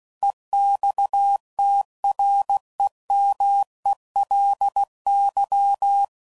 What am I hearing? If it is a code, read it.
Morse code, heard as a keyed tone, about 16 wpm, 800 Hz. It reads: EXTREMELY
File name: CW Code Groups @15WPM / Technical Subjects / AND Gate